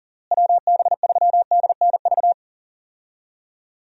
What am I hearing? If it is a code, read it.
Morse code, heard as a keyed tone, about 40 wpm, 700 Hz. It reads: WB3DNV
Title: USA Call Signs - 6 Characters - Repeated 2x - 40wpm